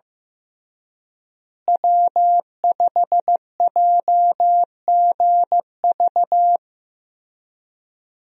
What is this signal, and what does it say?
Morse code, heard as a keyed tone, about 16 wpm, 700 Hz. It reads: W5JGV
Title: USA Call Signs - 5 Characters - Repeated 2x - 15wpm